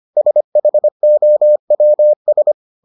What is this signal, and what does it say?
Morse code, heard as a keyed tone, about 25 wpm, 600 Hz. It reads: SHOWS